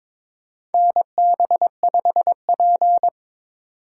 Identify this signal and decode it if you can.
Morse code, heard as a keyed tone, about 22 wpm, 700 Hz. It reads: NB5P
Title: USA Call Signs - 4 Characters - 3x Character Spacing - 22wpm